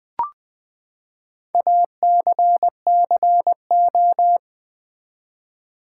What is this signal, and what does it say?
Morse code, heard as a keyed tone, about 19 wpm, 700 Hz. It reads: ACCO
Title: Build A Word - 4 Letters At A Time - 11-Letter Words - 20wpm